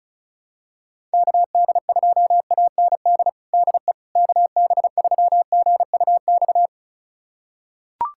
Morse code, heard as a keyed tone, about 36 wpm, 700 Hz. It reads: KD2AND DE KB3GUX
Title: USA Callsign DE Callsign - 6 Characters - 35wpm